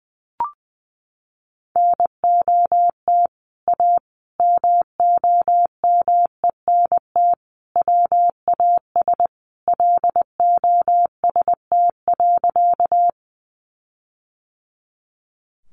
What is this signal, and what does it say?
Morse code, heard as a keyed tone, about 19 wpm, 700 Hz. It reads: NOT A MOMENT WAS LOST.
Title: Sentences from Top 1400 Words 20wpm